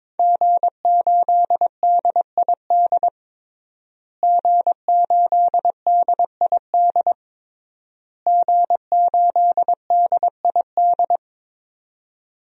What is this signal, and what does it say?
Morse code, heard as a keyed tone, about 22 wpm, 700 Hz. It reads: G8DID G8DID G8DID
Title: DXCC-100 Call Signs - 5 Characters - Repeated 3x - 22wpm